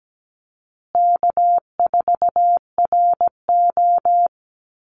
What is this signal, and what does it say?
Morse code, heard as a keyed tone, about 17 wpm, 700 Hz. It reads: K4RO
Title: Easy Call Signs Repeated 3x - 17wpm